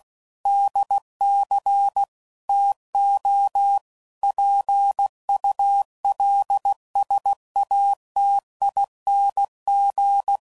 Morse code, heard as a keyed tone, about 16 wpm, 800 Hz. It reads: DC TO PULSATING
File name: CW Code Groups @15WPM / Technical Subjects / PwrSupply